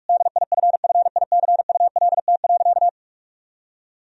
Morse code, heard as a keyed tone, about 47 wpm, 700 Hz. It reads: DIFFICULT.
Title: Sentences from Top 1000 Words - Review Part 2 - 45wpm